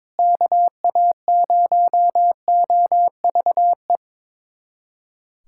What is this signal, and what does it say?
Morse code, heard as a keyed tone, about 21 wpm, 700 Hz. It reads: KA0OVE
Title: USA Callsign DE Callsign 22wpm